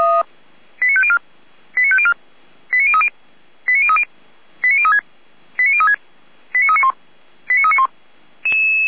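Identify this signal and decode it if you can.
Five-tone selective call (ZVEI), heard as quick runs of five tones, tones of about 70 ms. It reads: 89504 89504 89030 89030 89036 89036 89391 89391
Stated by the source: Five-tone ZVEI selective call received on SDR.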